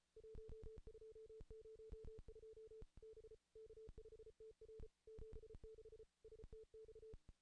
Morse code, heard as a keyed tone, about 34 wpm, 440 Hz. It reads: 1202 B K5TR 76 STX
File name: contests / ss-cw-k5tr-op-n5rz -1800